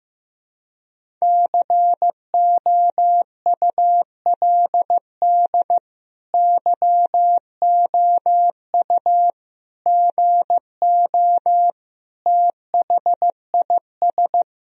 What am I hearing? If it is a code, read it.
Morse code, heard as a keyed tone, about 16 wpm, 700 Hz. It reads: COULD YOU GO THIS
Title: Sentences from Top 1000 Words - Review Part 4 - 15wpm